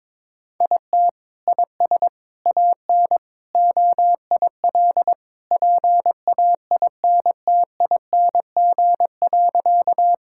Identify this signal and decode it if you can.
Morse code, heard as a keyed tone, about 22 wpm, 700 Hz. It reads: IT IS AN OIL PAINTING.
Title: Sentences from Top 1000 Words - Review Part 6 - 22wpm